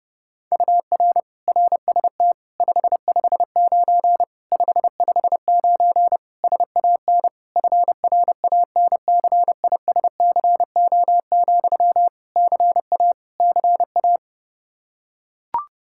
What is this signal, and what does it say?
Morse code, heard as a keyed tone, about 30 wpm, 700 Hz. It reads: UR RST 559 559 SAN FRANCISCO, CA CA
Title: Practice QSO - Exchange 1 - Beginning - 30wpm